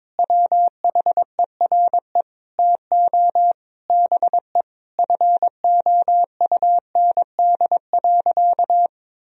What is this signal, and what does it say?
Morse code, heard as a keyed tone, about 22 wpm, 700 Hz. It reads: WHERE TO BE FOUND.
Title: Sentences from Top 1000 Words - Review Part 3 - 22wpm